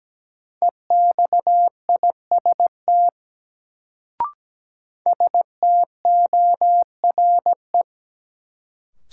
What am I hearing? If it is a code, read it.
Morse code, heard as a keyed tone, about 17 wpm, 700 Hz. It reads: EXIST STORE
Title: Sentences from Top 1000 Words - Review Part 5 - New Word Drill - 17wpm